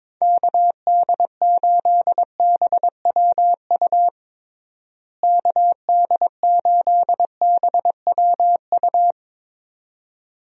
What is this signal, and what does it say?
Morse code, heard as a keyed tone, about 22 wpm, 700 Hz. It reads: KD8BWU KD8BWU
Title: USA Call Signs - 6 Characters - Repeated 2x - 22wpm